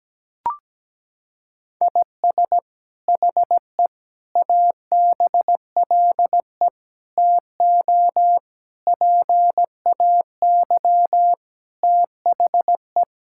Morse code, heard as a keyed tone, about 17 wpm, 700 Hz. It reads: IS HE ABLE TO PAY THE
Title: Sentences from Top 1000 Words - Review Part 5 - 17wpm